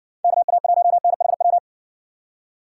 Morse code, heard as a keyed tone, about 56 wpm, 700 Hz. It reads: KA1NHW